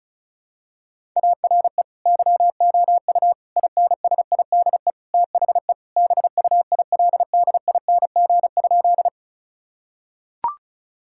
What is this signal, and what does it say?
Morse code, heard as a keyed tone, about 34 wpm, 700 Hz. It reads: ARE YOU INSIDE THE BUILDING?
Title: Sentences from Top 1000 Words - Review Part 6 - 35wpm